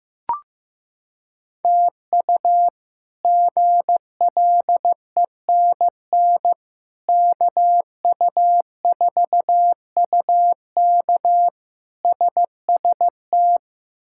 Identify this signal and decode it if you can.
Morse code, heard as a keyed tone, about 15 wpm, 700 Hz. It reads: TU GLENN KU4UK SST
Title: K1USN SST QSO Practice - 15wpm